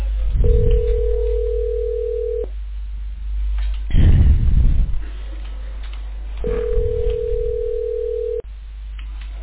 Ringback tone (North America), 440 + 480 Hz.